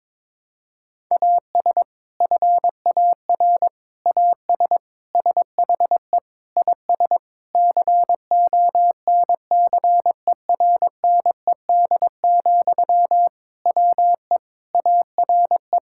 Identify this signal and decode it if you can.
Morse code, heard as a keyed tone, about 21 wpm, 700 Hz. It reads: AS FAR AS SHE IS CONCERNED, WE ARE
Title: Sentences from Top 1000 Words - Review Part 6 - 22wpm